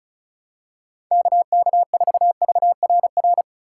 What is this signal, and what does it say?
Morse code, heard as a keyed tone, about 36 wpm, 700 Hz. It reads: KK4VRR
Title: USA Call Signs - 6 Characters - 4x Character Spacing - 35wpm